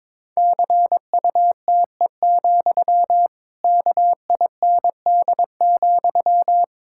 Morse code, heard as a keyed tone, about 22 wpm, 700 Hz. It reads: CUTE, KIND,
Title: Pangram Challenge v2 - 22wpm